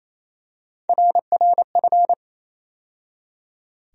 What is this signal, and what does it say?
Morse code, heard as a keyed tone, about 28 wpm, 700 Hz. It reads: RRF
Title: USA Call Sign Postfixes - 3 Letters 28wpm